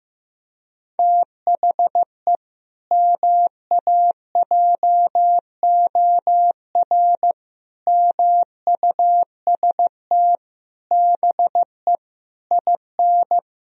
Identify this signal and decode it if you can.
Morse code, heard as a keyed tone, about 15 wpm, 700 Hz. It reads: THE MAJOR MUST BE IN
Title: Sentences from Top 1000 Words - Review Part 4 - 15wpm